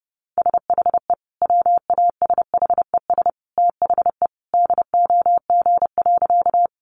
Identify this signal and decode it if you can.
Morse code, heard as a keyed tone, about 30 wpm, 700 Hz. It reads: SHE WASHES THE DOG.
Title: Easy Phrases 30wpm